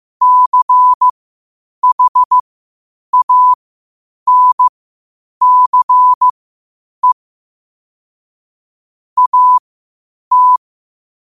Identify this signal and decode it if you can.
Morse code, heard as a keyed tone, about 15 wpm, spaced as for 8 wpm, 1000 Hz. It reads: CHANCE AT
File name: qotdfiles / 8wpm